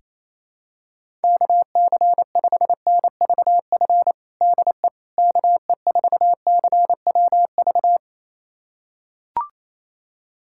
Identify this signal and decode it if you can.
Morse code, heard as a keyed tone, about 28 wpm, 700 Hz. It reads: KC5NVF DE KE4CWV
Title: USA Callsign DE Callsign 28wpm